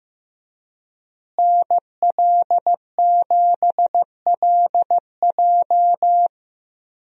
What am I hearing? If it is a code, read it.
Morse code, heard as a keyed tone, about 16 wpm, 700 Hz. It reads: NL7LJ
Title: USA Call Signs - 5 Characters - Repeated 2x - 15wpm